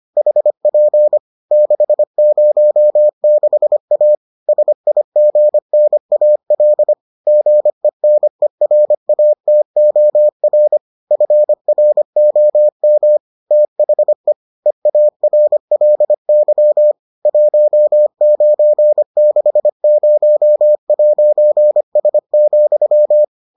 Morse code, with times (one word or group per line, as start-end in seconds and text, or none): 0.2-1.2 HP
1.5-4.2 606A
4.5-6.9 SIGNAL
7.3-10.8 GENERATOR
11.1-13.2 FROM
13.5-14.3 THE
14.7-16.9 EARLY
17.3-23.2 1960'S,